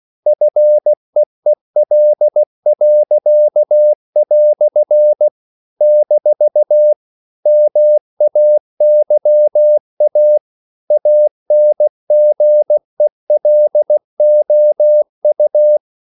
Morse code, heard as a keyed tone, about 16 wpm, 600 Hz. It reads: FEEL." - MAYA ANGELOU
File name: CW / Practice / EnglishQuotations0216